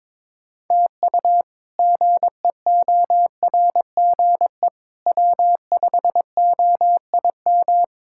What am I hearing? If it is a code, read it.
Morse code, heard as a keyed tone, about 22 wpm, 700 Hz. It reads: TU GEORGE W5OIM
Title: K1USN SST QSO Practice - 22wpm